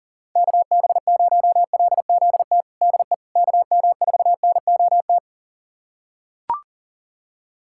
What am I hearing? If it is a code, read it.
Morse code, heard as a keyed tone, about 37 wpm, 690 Hz. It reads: KB0LZT DE KM4NOT
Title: USA Callsign DE Callsign - 6 Characters - 40wpm